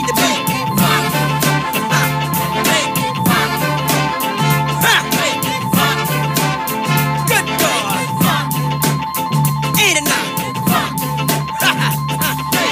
Two-tone teleprinter signal, about 170 Hz shift.